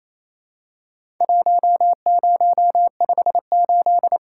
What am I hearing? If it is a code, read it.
Morse code, heard as a keyed tone, about 28 wpm, 700 Hz. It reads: 1058
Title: Worked All Europe DX Contest - QTC Exchange - 28wpm